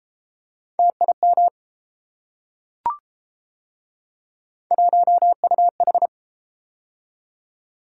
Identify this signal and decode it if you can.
Morse code, heard as a keyed tone, about 31 wpm, 700 Hz. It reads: TIM 1UH